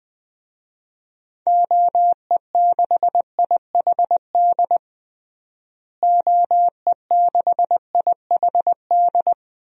Morse code, heard as a keyed tone, about 20 wpm, 700 Hz. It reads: OE6IHD OE6IHD
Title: DXCC-100 Call Signs - 6 Characters - Repeated 2x - 20wpm